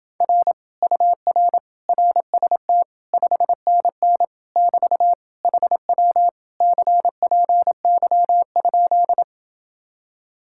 Morse code, heard as a keyed tone, about 29 wpm, 700 Hz. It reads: R UR RST 5NN <BT> HW CPY?